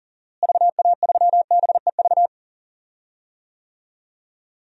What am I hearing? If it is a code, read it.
Morse code, heard as a keyed tone, about 40 wpm, 700 Hz. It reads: VA3BEV
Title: DXCC-100 Call Signs - 6 Characters - Repeated 3x - 40wpm